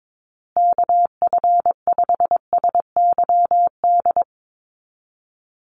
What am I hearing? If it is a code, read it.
Morse code, heard as a keyed tone, about 22 wpm, 700 Hz. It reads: KF5SYD
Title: USA Call Signs Repeated 2x 22wpm